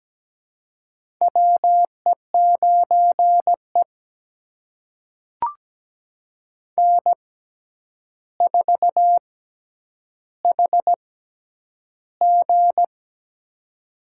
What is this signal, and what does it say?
Morse code, heard as a keyed tone, about 17 wpm, spaced as for 9 wpm, 700 Hz. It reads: WE9E N4HG
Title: USA Call Signs - 4 Characters - 6x Character Spacing - 17wpm